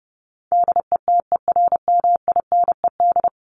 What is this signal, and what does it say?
Morse code, heard as a keyed tone, about 30 wpm, 700 Hz. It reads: DETERMINED
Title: ED Suffix - Top 100 Words with Base Words 30wpm